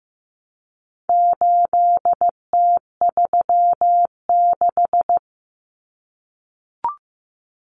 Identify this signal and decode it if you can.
Morse code, heard as a keyed tone, about 15 wpm, 700 Hz. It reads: 8T36
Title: Common Cut Numbers - 4-Digit 15wpm